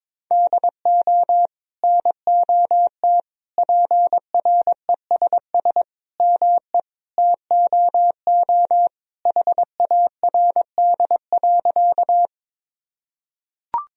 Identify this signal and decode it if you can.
Morse code, heard as a keyed tone, about 21 wpm, 700 Hz. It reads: DO NOT PRESS ME TOO HARD.
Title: Sentences from Top 1000 Words - Review Part 1 - 22wpm